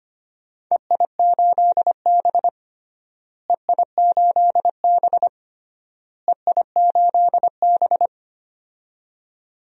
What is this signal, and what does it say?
Morse code, heard as a keyed tone, about 25 wpm, 700 Hz. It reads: EI8B EI8B EI8B
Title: DXCC-100 Call Signs - 4 Characters - Repeated 3x - 25wpm